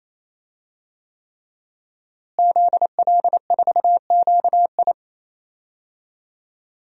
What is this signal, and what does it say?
Morse code, heard as a keyed tone, about 28 wpm, 700 Hz. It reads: ZL4QI
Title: DXCC-100 Call Signs - 5 Characters - Repeated 3x - 28wpm